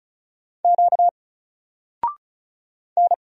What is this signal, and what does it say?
Morse code, heard as a keyed tone, about 35 wpm, 700 Hz. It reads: Q N
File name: Single-Letter-Number-35wpm